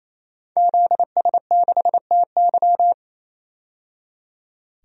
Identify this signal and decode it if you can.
Morse code, heard as a keyed tone, about 28 wpm, 700 Hz. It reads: ZS6TY
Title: DXCC-100 Call Signs - 5 Characters - 28wpm